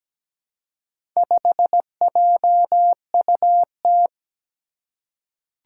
Morse code, heard as a keyed tone, about 17 wpm, 700 Hz. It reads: 5JUT
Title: Four Challenging Letter-Number Combinations 17wpm